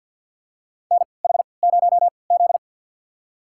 Morse code, heard as a keyed tone, about 52 wpm, spaced as for 29 wpm, 700 Hz. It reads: NH0Z
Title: USA Call Signs - 3x Character Spacing - 50wpm